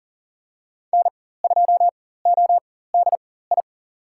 Morse code, heard as a keyed tone, about 37 wpm, spaced as for 18 wpm, 700 Hz. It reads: N2ODI
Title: USA Call Signs - 5 Characters - 4x Character Spacing - 40wpm